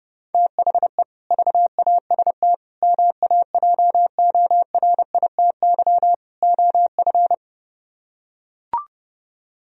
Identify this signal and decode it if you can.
Morse code, heard as a keyed tone, about 32 wpm, 700 Hz. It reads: THE VAST MAJORITY OF